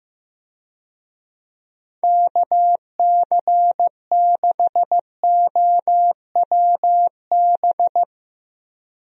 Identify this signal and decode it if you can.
Morse code, heard as a keyed tone, about 15 wpm, 700 Hz. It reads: KC6OWB
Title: USA Call Signs - 6 Characters - Repeated 3x - 15wpm